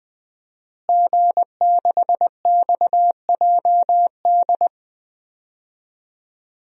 Morse code, heard as a keyed tone, about 20 wpm, 700 Hz. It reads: G6XJD
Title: DXCC-100 Call Signs - 5 Characters - Repeated 2x - 20wpm